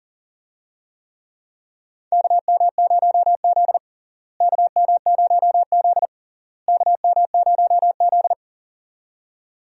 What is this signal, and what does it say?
Morse code, heard as a keyed tone, about 40 wpm, 700 Hz. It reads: KM0Z KM0Z KM0Z
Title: USA Call Signs - 4 Characters - Repeated 3x - 40wpm